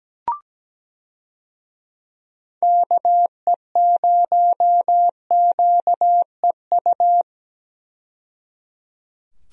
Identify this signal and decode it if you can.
Morse code, heard as a keyed tone, about 17 wpm, 700 Hz. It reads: KE0QEU
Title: USA Call Signs - 6 Characters - 17wpm